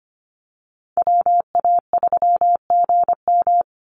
Morse code, heard as a keyed tone, about 25 wpm, 700 Hz. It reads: WA3GM
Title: Call Signs Repeated 2x - 25wpm